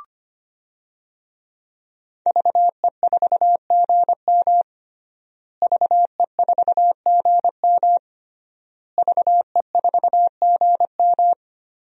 Morse code, heard as a keyed tone, about 25 wpm, 700 Hz. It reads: VE4GM VE4GM VE4GM
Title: DXCC-100 Call Signs - 5 Characters - Repeated 3x - 25wpm